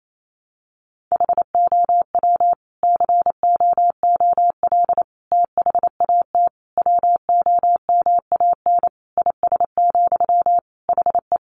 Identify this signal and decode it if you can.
Morse code, heard as a keyed tone, about 28 wpm, 700 Hz. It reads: HOW COOL THAT WOMAN IS, HE
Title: Sentences from Top 1200 Words 28wpm